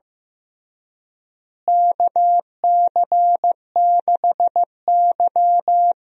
Morse code, heard as a keyed tone, about 15 wpm, 700 Hz. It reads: KC6Y
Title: USA Call Signs - 4 Characters - Repeated 3x - 15wpm